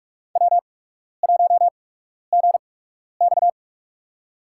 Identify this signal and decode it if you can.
Morse code, heard as a keyed tone, about 43 wpm, spaced as for 12 wpm, 700 Hz. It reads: W1GX
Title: USA Call Signs - 4 Characters - 8x Character Spacing - 45wpm